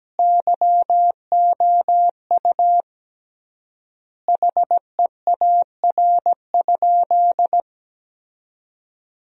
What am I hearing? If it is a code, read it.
Morse code, heard as a keyed tone, about 17 wpm, 700 Hz. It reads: YOU HEAR?
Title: 3x Word-Spacing - Sentences from Top 300 Words - 17wpm